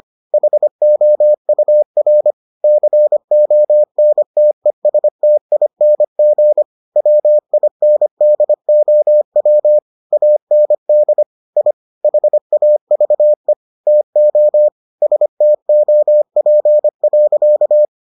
Morse code, heard as a keyed tone, about 25 wpm, 600 Hz. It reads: HOUR CONTESTING WINDOW AND I HAVE TO STOP.